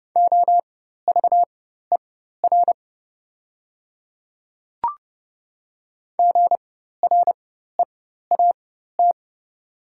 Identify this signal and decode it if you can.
Morse code, heard as a keyed tone, about 30 wpm, spaced as for 14 wpm, 690 Hz. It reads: OVER GREAT
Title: 4x Character Spacing - Top 200 Words - 30wpm